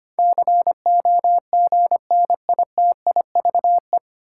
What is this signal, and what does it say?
Morse code, heard as a keyed tone, about 25 wpm, 700 Hz. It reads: COGNITIVE
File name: Nine Letter Words - 25wpm